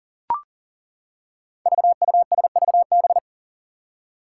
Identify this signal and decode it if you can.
Morse code, heard as a keyed tone, about 38 wpm, 700 Hz. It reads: VUSVB
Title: Five Challenging Letter-Number Combinations 40wpm